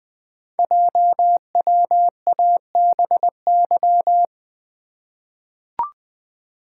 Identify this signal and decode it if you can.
Morse code, heard as a keyed tone, about 20 wpm, 700 Hz. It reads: JWABY
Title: Code Groups - 5 Letters - 20wpm